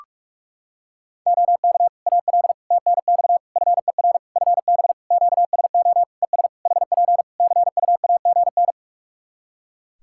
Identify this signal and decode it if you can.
Morse code, heard as a keyed tone, about 46 wpm, 700 Hz. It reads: OK AL TNX FER FB QSO ES HP CUAGN
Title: Practice QSO - Exchange 4 - Ending - 45wpm